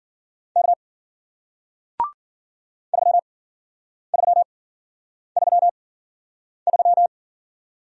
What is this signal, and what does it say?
Morse code, heard as a keyed tone, about 48 wpm, spaced as for 9 wpm, 700 Hz. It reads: K 3333
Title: Speed-Racing Single Letter-Number 40wpm